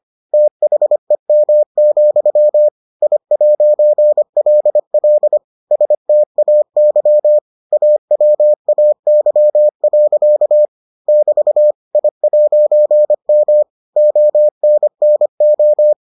Morse code, heard as a keyed tone, about 25 wpm, 600 Hz. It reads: THEM, I'LL STAY AWAY. <BT> I'M ONNO